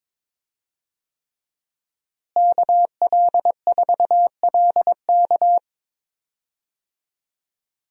Morse code, heard as a keyed tone, about 22 wpm, 700 Hz. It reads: KL4LK
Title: USA Call Signs - 5 Characters - Repeated 3x - 22wpm